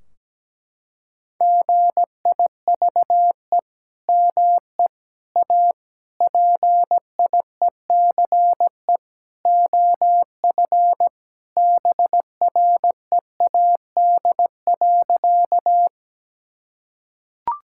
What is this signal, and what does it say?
Morse code, heard as a keyed tone, about 17 wpm, 700 Hz. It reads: GIVE ME A PIECE OF BREAD.
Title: Sentences from Top 1600 Words 17wpm